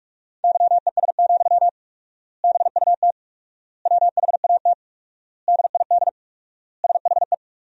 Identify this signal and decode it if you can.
Morse code, heard as a keyed tone, about 47 wpm, 700 Hz. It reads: YES, BUT WHAT DID SHE
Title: 4x Word-Spacing - Sentences from Top 500 Words - 45wpm